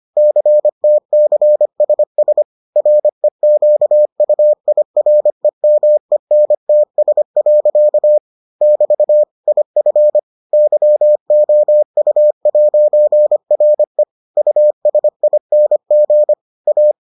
Morse code, heard as a keyed tone, about 25 wpm, 600 Hz. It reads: CTCSS REQUIREMENTS. <BT> IF YOU'RE USING A